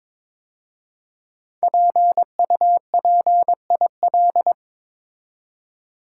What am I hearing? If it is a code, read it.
Morse code, heard as a keyed tone, about 22 wpm, 700 Hz. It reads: PUPIL
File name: 5 Letter Words - First 3 Letters Given - 22wpm